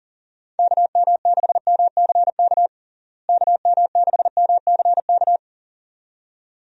Morse code, heard as a keyed tone, about 40 wpm, 700 Hz. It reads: KM6MCK KM6MCK
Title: USA Call Signs - 6 Characters - Repeated 2x - 40wpm